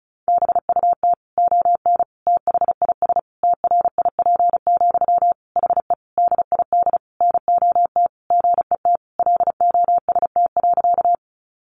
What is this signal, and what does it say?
Morse code, heard as a keyed tone, about 34 wpm, 700 Hz. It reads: BUT ON THIS TRIP, HE DID NOT GET LOST.
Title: Sentences from Top 1400 Words 35wpm